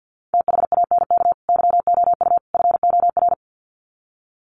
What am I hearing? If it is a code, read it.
Morse code, heard as a keyed tone, about 50 wpm, 700 Hz. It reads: THANK YOU FOR
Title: Sets of 3 Words 50wpm